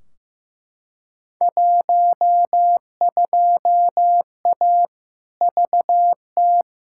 Morse code, heard as a keyed tone, about 15 wpm, 700 Hz. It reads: 12A VT
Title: Field Day Exchanges Encore - 15wpm